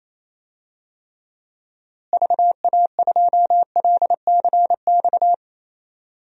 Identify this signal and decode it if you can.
Morse code, heard as a keyed tone, about 28 wpm, 700 Hz. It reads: VA2LCX
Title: DXCC-100 Call Signs - 6 Characters - Repeated 2x - 28wpm